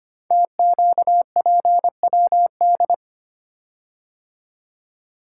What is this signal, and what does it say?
Morse code, heard as a keyed tone, about 25 wpm, 700 Hz. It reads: TQPWD
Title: Code Groups - 5 Letters - 25wpm